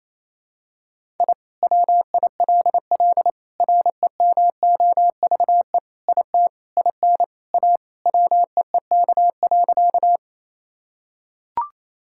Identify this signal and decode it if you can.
Morse code, heard as a keyed tone, about 27 wpm, 700 Hz. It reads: I WILL REMOVE IT IN A WEEK.
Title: Sentences from Top 1000 Words - Review Part 5 - 28wpm